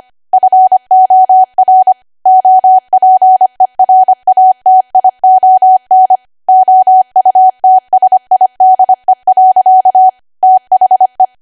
Morse code, heard as a keyed tone, about 25 wpm, 750 Hz. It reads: FOR OPERATION OUTSIDE. THE
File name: cwa / W1AW-Level3 / 25WPM